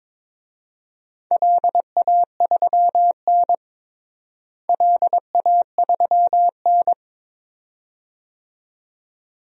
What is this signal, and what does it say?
Morse code, heard as a keyed tone, about 22 wpm, 700 Hz. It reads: LA3N LA3N
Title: DXCC-100 Call Signs - 4 Characters - Repeated 2x - 22wpm